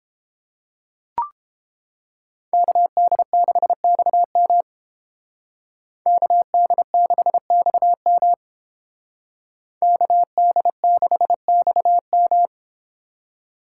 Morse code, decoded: KD6XM KD6XM KD6XM

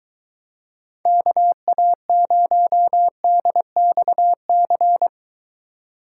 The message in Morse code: KA0DXC

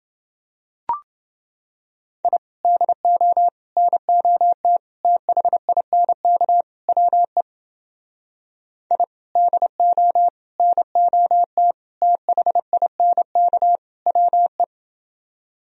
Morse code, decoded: I DO NOT THINK WE I DO NOT THINK WE

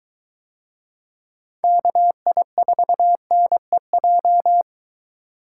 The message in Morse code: KI4NEJ

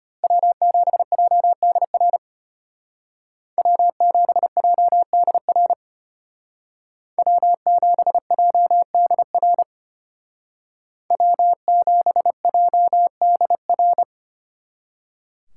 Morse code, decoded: W7JDR W7JDR W7JDR W7JDR

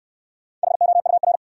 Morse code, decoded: HOUR

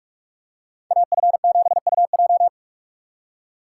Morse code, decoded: AF7UJ

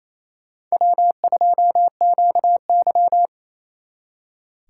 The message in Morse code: W2QY

